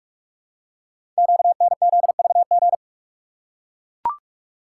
Morse code, heard as a keyed tone, about 44 wpm, 690 Hz. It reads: QNZVG
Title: Code Groups - 5 Letters - 45wpm